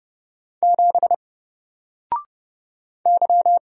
Morse code, heard as a keyed tone, about 30 wpm, 700 Hz. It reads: 7 Y
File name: Single-Letter-Number-30wpm